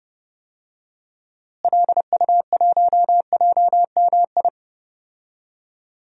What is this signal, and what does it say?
Morse code, heard as a keyed tone, about 32 wpm, 700 Hz. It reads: LU1JMI
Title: DXCC-100 Call Signs - 6 Characters - Repeated 2x - 30wpm